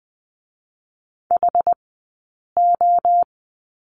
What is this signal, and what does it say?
Morse code, heard as a keyed tone, about 20 wpm, 700 Hz. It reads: H O
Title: 059 Lesson 11d - Letter h - Two Characters 20wpm